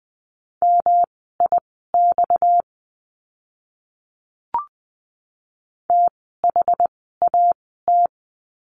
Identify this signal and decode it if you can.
Morse code, heard as a keyed tone, about 20 wpm, spaced as for 15 wpm, 700 Hz. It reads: MIX THAT
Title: 230 Lesson 37f - Letter x - Words 20wpm